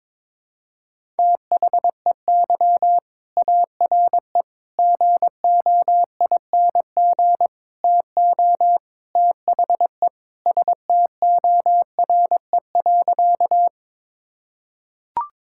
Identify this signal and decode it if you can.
Morse code, heard as a keyed tone, about 22 wpm, 700 Hz. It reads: THEY ARE GOING TO THE STORE.
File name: practice / 22wpm / Sentences with 4-Word Phrases - 22wpm